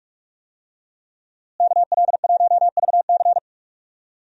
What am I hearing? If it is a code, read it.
Morse code, heard as a keyed tone, about 46 wpm, 700 Hz. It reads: KL1VC
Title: USA Call Signs - 5 Characters - Repeated 2x - 45wpm